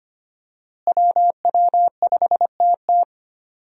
Morse code, heard as a keyed tone, about 25 wpm, 700 Hz. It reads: WW5TT
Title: USA Call Signs - 5 Characters - Repeated 3x - 25wpm